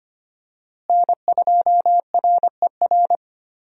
Morse code, heard as a keyed tone, about 25 wpm, 700 Hz. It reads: N2RER
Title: USA Call Signs - 5 Characters - Repeated 2x - 25wpm